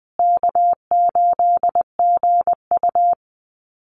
Morse code, heard as a keyed tone, about 20 wpm, 700 Hz. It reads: K8GU
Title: Easy Call Signs Repeated 3x - 20wpm